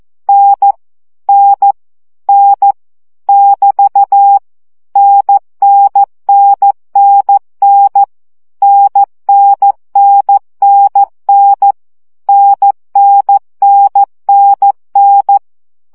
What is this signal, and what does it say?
Morse code, heard as a keyed tone, about 14 wpm, 800 Hz. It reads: N N N <BT> NNNNN NNNNN NNNNN
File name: CW gyakorlas n 12 baud